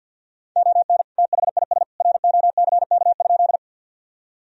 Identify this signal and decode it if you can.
Morse code, heard as a keyed tone, about 50 wpm, 700 Hz. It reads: ON THIS ROCK?
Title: Sentences from Top 1000 Words - Review Part 2 - 50wpm